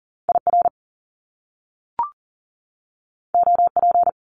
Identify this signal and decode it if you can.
Morse code, heard as a keyed tone, about 40 wpm, 700 Hz. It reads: IR OP
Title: Top Two Letter Combinations in Words 40wpm